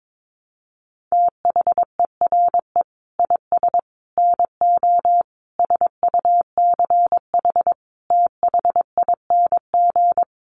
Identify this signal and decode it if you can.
Morse code, heard as a keyed tone, about 22 wpm, 700 Hz. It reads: THERE IS NO SUCH THING
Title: Sets of 5 Words Encore 22wpm